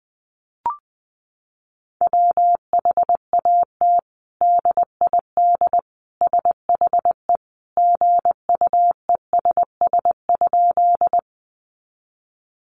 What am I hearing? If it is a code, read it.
Morse code, heard as a keyed tone, about 20 wpm, 700 Hz. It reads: WHAT DID SHE GUESS?
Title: Sentences from Top 1500 Words 20wpm